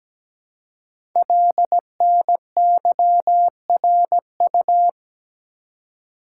Morse code, heard as a keyed tone, about 17 wpm, 700 Hz. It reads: LNYRU
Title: Code Groups - 5 Letter:Numbers - 17wpm